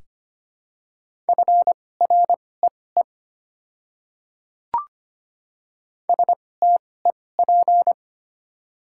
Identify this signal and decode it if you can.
Morse code, heard as a keyed tone, about 25 wpm, spaced as for 18 wpm, 700 Hz. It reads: FREE STEP
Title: 2x Character Spacing - Top 500 Words - 25wpm